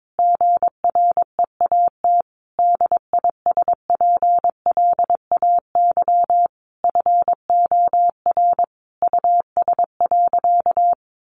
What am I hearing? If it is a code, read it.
Morse code, heard as a keyed tone, about 22 wpm, 700 Hz. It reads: GREAT DISPLAY FOR US.
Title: Sentences from Top 1200 Words 22wpm